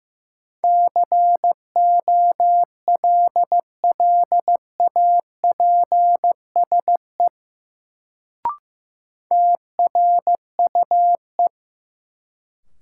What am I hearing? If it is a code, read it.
Morse code, heard as a keyed tone, about 15 wpm, 700 Hz. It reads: COLLAPSE TRUE
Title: Sets of 5 Words Encore - New Word Drill - 15wpm